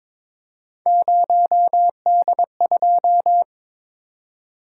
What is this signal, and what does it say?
Morse code, heard as a keyed tone, about 22 wpm, 700 Hz. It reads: 0D2
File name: practice / 22wpm / Three Letter-Number Combinations - 22wpm